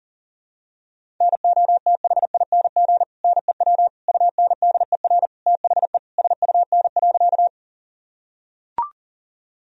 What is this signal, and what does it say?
Morse code, heard as a keyed tone, about 40 wpm, 700 Hz. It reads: NOTHING NEW UNDER THE SUN.
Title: Sentences from Top 1000 Words - Review Part 1 - 40wpm